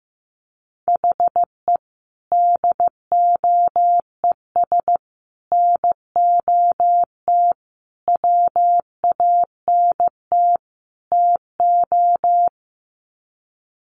Morse code, heard as a keyed tone, about 14 wpm, 700 Hz. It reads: HE DOES NOT WANT TO
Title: Sets of 5 Words 15wpm